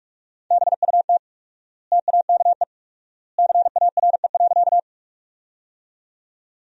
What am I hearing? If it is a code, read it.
Morse code, heard as a keyed tone, about 46 wpm, 700 Hz. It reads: BUT TAKE CARE.
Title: 4x Word-Spacing - Sentences from Top 300 Words - 45wpm